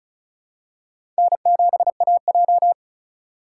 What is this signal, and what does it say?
Morse code, heard as a keyed tone, about 34 wpm, 700 Hz. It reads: N7AJ
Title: USA Call Signs - 4x Character Spacing - 35wpm